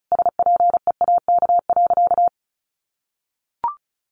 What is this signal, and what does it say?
Morse code, heard as a keyed tone, about 34 wpm, 700 Hz. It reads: SPEAK.
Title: Sentences from Top 1500 Words 35wpm